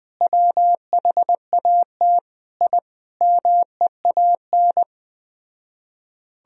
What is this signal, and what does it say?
Morse code, heard as a keyed tone, about 20 wpm, 700 Hz. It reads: WHAT I MEAN